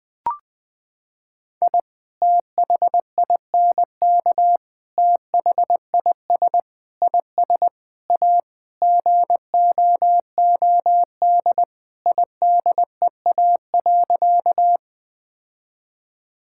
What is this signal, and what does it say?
Morse code, heard as a keyed tone, about 20 wpm, 700 Hz. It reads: I THINK THIS IS A GOOD IDEA.
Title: Sentences with 5-Word Phrases - 20wpm